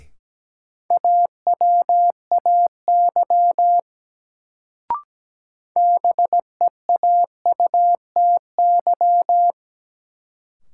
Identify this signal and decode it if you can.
Morse code, heard as a keyed tone, about 17 wpm, 700 Hz. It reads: AWAY BEAUTY
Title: Sentences from Top 600 Words - New Word Drill - 17wpm